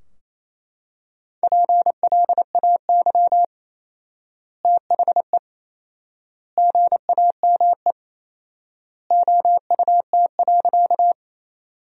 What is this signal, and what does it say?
Morse code, heard as a keyed tone, about 28 wpm, 700 Hz. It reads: PLAY THE GAME OUT.
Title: 4x Word-Spacing - Sentences from Top 500 Words - 28wpm